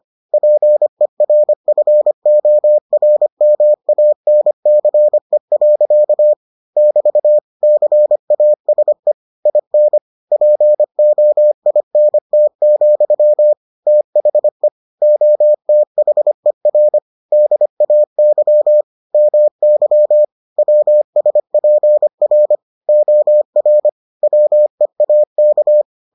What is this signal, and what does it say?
Morse code, heard as a keyed tone, about 25 wpm, 600 Hz. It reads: PERFORMANCE. <BT> CASE IN POINT, THE OTHER DAY MY WSPR OR WEAK